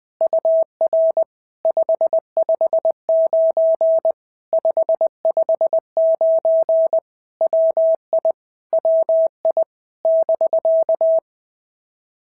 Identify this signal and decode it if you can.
Morse code, heard as a keyed tone, about 20 wpm, 650 Hz. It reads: UR 559 559 WI WI <BK>
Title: POTA QSO Practice - 20wpm